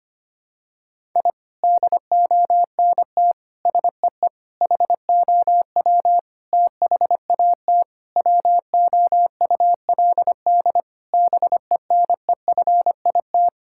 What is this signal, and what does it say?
Morse code, heard as a keyed tone, about 25 wpm, 700 Hz. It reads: I DONT SEE HOW THAT WOULD BENEFIT
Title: Sentences from Top 1000 Words - Review Part 6 - 25wpm